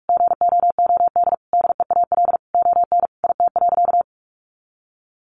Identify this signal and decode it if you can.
Morse code, heard as a keyed tone, about 44 wpm, 700 Hz. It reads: GOOD DEAL ON IT.
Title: Sentences from Top 1300 Words 45wpm